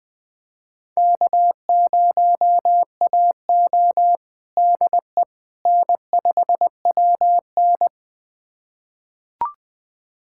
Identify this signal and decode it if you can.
Morse code, heard as a keyed tone, about 20 wpm, 700 Hz. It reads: K0AO DE N5WN
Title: USA Callsign DE Callsign - 4 Characters - 20wpm